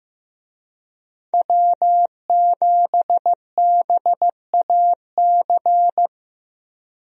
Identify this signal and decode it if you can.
Morse code, heard as a keyed tone, about 15 wpm, 700 Hz. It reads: W7BAC
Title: USA Call Signs - 5 Characters - Repeated 3x - 15wpm